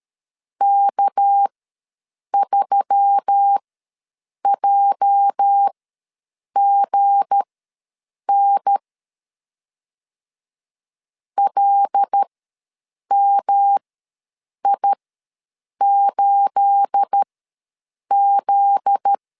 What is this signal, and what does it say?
Morse code, heard as a keyed tone, about 13 wpm, spaced as for 7 wpm, 800 Hz. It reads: K3JGN LMI8Z
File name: LEZIONI / 9 28 ws 12wpm 3354